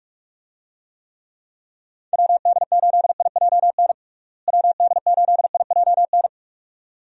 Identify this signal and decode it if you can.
Morse code, heard as a keyed tone, about 43 wpm, 700 Hz. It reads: WD8IJN WD8IJN
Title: USA Call Signs - 6 Characters - Repeated 2x - 45wpm